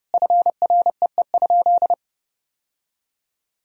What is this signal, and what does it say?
Morse code, heard as a keyed tone, about 32 wpm, 700 Hz. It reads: FREE?
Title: Sentences from Top 1000 Words - Review Part 3 - 30wpm